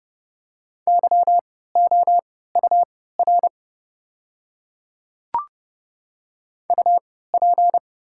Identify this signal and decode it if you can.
Morse code, heard as a keyed tone, about 30 wpm, spaced as for 17 wpm, 700 Hz. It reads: YOUR UP
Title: 3x Character Spacing - Top 100 Words - 30wpm